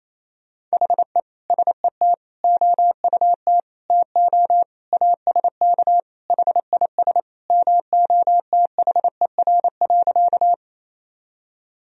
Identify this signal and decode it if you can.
Morse code, heard as a keyed tone, about 28 wpm, 700 Hz. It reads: HE SET OUT TO ASK HIS MOTHER.
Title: Sentences from Top 1000 Words - Review Part 1 - 28wpm